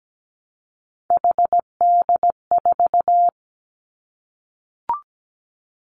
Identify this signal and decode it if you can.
Morse code, heard as a keyed tone, about 17 wpm, 700 Hz. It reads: HD4
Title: Three Challenging Letter-Number Combinations 17wpm